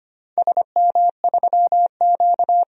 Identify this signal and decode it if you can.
Morse code, heard as a keyed tone, about 25 wpm, 700 Hz. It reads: SM3Q
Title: DXCC-100 Call Signs - 4 Characters - 25wpm